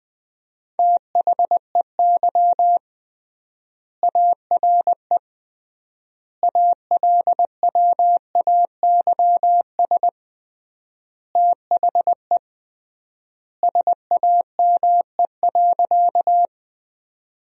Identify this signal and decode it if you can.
Morse code, heard as a keyed tone, about 20 wpm, 700 Hz. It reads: THEY ARE ALWAYS THE SAME.
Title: 3x Word-Spacing - Sentences from Top 300 Words - 20wpm